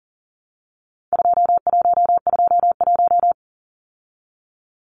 Morse code, heard as a keyed tone, about 40 wpm, 700 Hz. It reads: 2121
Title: Numbers - 4-Digit 40wpm